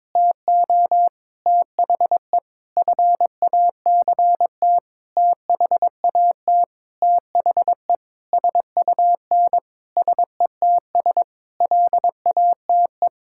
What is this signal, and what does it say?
Morse code, heard as a keyed tone, about 21 wpm, 700 Hz. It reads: TO THE FACT THAT THE SUN SETS LATE
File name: practice / 22wpm / Sentences with 8-Word Phrases - 22wpm